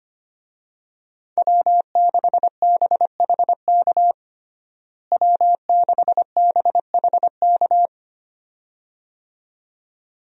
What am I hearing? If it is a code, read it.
Morse code, heard as a keyed tone, about 25 wpm, 700 Hz. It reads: W6BHK W6BHK
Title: USA Call Signs - 5 Characters - Repeated 2x - 25wpm